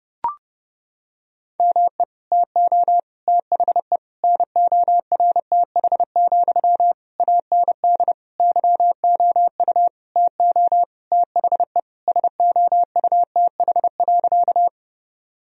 Morse code, heard as a keyed tone, about 30 wpm, 700 Hz. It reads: ME TO THE NORTH, AND YOU TO THE SOUTH.
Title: Sentences from Top 1000 Words - Review Part 2 - 30wpm